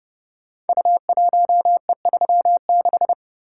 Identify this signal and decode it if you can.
Morse code, heard as a keyed tone, about 30 wpm, 700 Hz. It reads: U1E36
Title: Code Groups - 5 Letter:Numbers - 30wpm